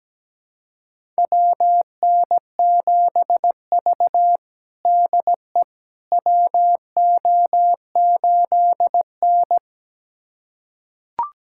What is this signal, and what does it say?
Morse code, heard as a keyed tone, about 17 wpm, 700 Hz. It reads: WN7V DE WO8N
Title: USA Callsign DE Callsign - 4 Characters - 17wpm